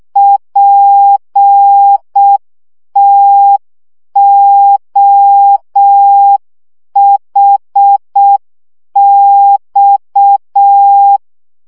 Morse code, heard as a keyed tone, about 6 wpm, 800 Hz. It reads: PTOHX
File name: morze vizsga 2011-1 5